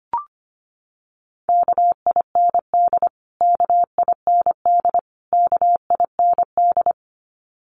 Morse code, heard as a keyed tone, about 25 wpm, 700 Hz. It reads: KIND KIND KIND
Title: Top 300 Words Repeated 25wpm